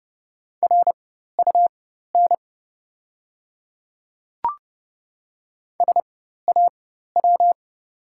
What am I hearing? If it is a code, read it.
Morse code, heard as a keyed tone, about 30 wpm, spaced as for 14 wpm, 700 Hz. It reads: RUN SAW
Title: 4x Character Spacing - Top 300 Words - 30wpm